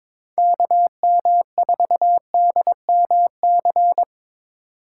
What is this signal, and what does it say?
Morse code, heard as a keyed tone, about 26 wpm, 700 Hz. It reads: KM4DMC
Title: Speed Racing - USA 2x3 Call Signs 22wpm